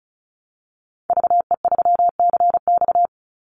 Morse code, heard as a keyed tone, about 34 wpm, 700 Hz. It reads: VE3CX
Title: Call Signs - 35wpm